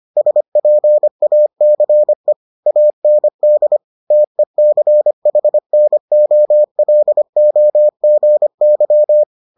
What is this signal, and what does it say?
Morse code, heard as a keyed tone, about 25 wpm, 600 Hz. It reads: SPACE AND TECHNOLOGY